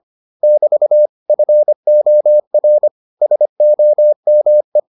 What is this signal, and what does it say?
Morse code, heard as a keyed tone, about 25 wpm, 600 Hz. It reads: <BT> FOR SOME